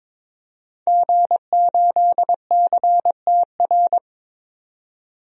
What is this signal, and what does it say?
Morse code, heard as a keyed tone, about 21 wpm, 700 Hz. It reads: G8CTR
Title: DXCC-100 Call Signs - 5 Characters - 22wpm